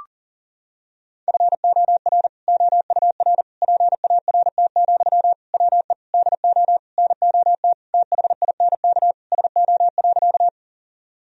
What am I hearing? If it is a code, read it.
Morse code, heard as a keyed tone, about 40 wpm, 700 Hz. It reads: FOR OUR PART, WE DO NOT THINK SO.
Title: Sentences from Top 1000 Words - Review Part 1 - 40wpm